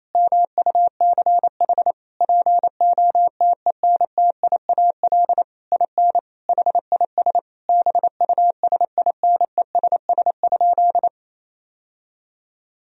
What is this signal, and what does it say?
Morse code, heard as a keyed tone, about 28 wpm, 700 Hz. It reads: MUCH POTENTIAL IN HIS BUSINESS?
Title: Sentences from Top 1000 Words - Review Part 4 - 28wpm